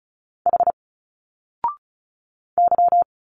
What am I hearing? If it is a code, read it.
Morse code, heard as a keyed tone, about 36 wpm, 700 Hz. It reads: H Y
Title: Single Letters 35wpm